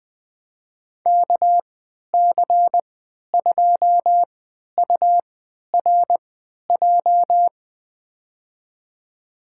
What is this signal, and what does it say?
Morse code, heard as a keyed tone, about 20 wpm, spaced as for 11 wpm, 700 Hz. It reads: KC2URJ
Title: USA Call Signs - 6 Characters - 3x Character Spacing - 20wpm